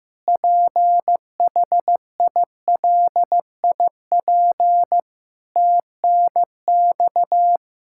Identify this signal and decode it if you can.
Morse code, heard as a keyed tone, about 15 wpm, 700 Hz. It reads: PHILIP TNX
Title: Practice QSO - Exchange 3 - More Info - 15wpm